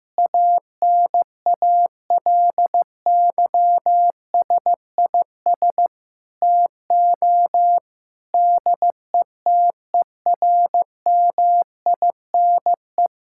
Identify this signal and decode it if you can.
Morse code, heard as a keyed tone, about 15 wpm, 700 Hz. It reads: ANALYSIS TO DETERMINE
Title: Sentences from Top 1000 Words - Review Part 5 - 15wpm